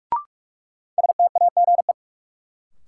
Morse code, heard as a keyed tone, about 44 wpm, 700 Hz. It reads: STAGE
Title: Sentences from Top 1200 Words - New Word Drill - 45wpm